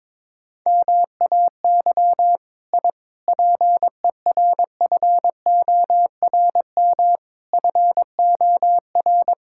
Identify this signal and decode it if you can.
Morse code, heard as a keyed tone, about 22 wpm, 700 Hz. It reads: MAY I PERFORM FOR
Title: Sentences from Top 1000 Words - Review Part 5 - 22wpm